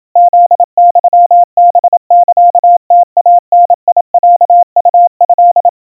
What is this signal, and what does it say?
Morse code (Japanese Wabun code), heard as a keyed tone, about 27 wpm, 700 Hz. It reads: フユハサムイダロウト
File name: kawa / Morse sample / kamyu 100 cpm 700 Hz Crispy